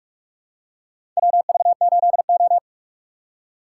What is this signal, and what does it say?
Morse code, heard as a keyed tone, about 46 wpm, 700 Hz. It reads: WV8O
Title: USA Call Signs - 3x Character Spacing - 45wpm